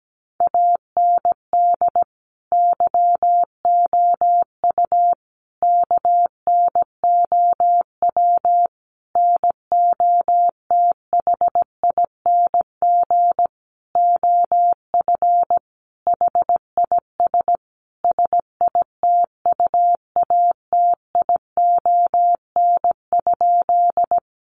Morse code, heard as a keyed tone, about 17 wpm, 700 Hz. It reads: AND YOU KNOW NOTHING OF HIS SITUATION?
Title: Sentences from Top 1200 Words 17wpm